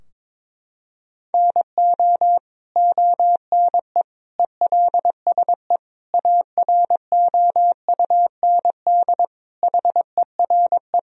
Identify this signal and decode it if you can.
Morse code, heard as a keyed tone, about 22 wpm, 700 Hz. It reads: NO ONE ELSE AROUND HERE
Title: Sentences from Top 1000 Words - Review Part 3 - 22wpm